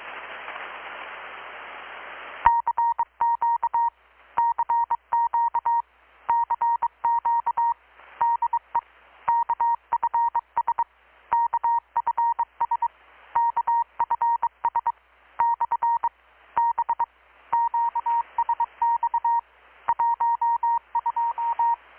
Morse code, heard as a keyed tone, about 22 wpm, 940 Hz. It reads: CQ CQ CQ DE KFS KFS KFS / B QSX 12
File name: blog / swl / 6-abril-2019-12695-USB-KPS